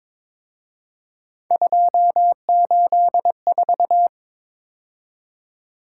Morse code, heard as a keyed tone, about 26 wpm, 700 Hz. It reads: 284
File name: Speed-Racing - Three Letter-Number Combinations - 22wpm